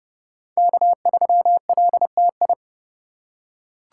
Morse code, heard as a keyed tone, about 30 wpm, 700 Hz. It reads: K3LTI
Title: USA Call Signs - 5 Characters - 3x Character Spacing - 30wpm